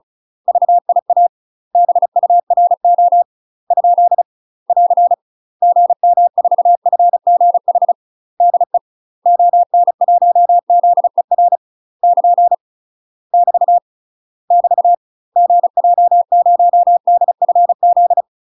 Morse code, heard as a keyed tone, about 34 wpm, 700 Hz. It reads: VIA BURO ? <AR> GM4FGH DE ON1ZER <KN> <BT> <BT> GJ0DFZ
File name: UK MASTER QSOs TRAINING / 35WPM / UK MASTER QSOs 0040